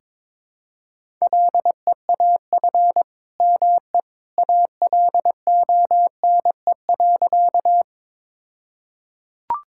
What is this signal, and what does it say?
Morse code, heard as a keyed tone, about 22 wpm, 700 Hz. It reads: LEAF ME ALONE.
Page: morse-code-ninja-practice-22wpm